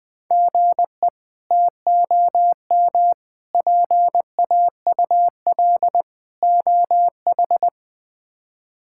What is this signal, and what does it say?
Morse code, heard as a keyed tone, about 21 wpm, 700 Hz. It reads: GE TOM PAUL OH
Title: K1USN SST QSO Practice - 20wpm